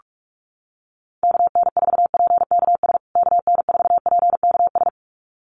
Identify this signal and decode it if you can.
Morse code, heard as a keyed tone, about 44 wpm, 700 Hz. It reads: KN4PKS KN4PKS
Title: USA Call Signs Repeated 2x 45wpm